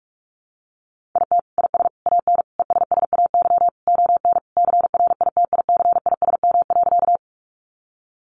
Morse code, heard as a keyed tone, about 44 wpm, 700 Hz. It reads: IT IS AN ESSAY ON CRITICISM.
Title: Sentences from Top 1500 Words 45wpm